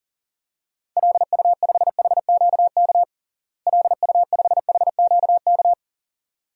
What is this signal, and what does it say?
Morse code, heard as a keyed tone, about 40 wpm, 700 Hz. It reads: LU5HQK LU5HQK
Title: DXCC-100 Call Signs - 6 Characters - Repeated 2x - 40wpm